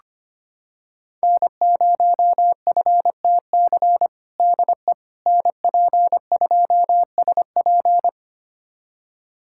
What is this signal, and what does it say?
Morse code, heard as a keyed tone, about 25 wpm, 700 Hz. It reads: N0FTC DE NP2SP
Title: USA Callsign DE Callsign - 5 Characters - 25wpm